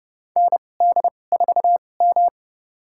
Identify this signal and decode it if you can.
Morse code, heard as a keyed tone, about 31 wpm, spaced as for 22 wpm, 700 Hz. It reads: ND4M
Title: USA Call Signs - 4 Characters - 2x Character Spacing - 30wpm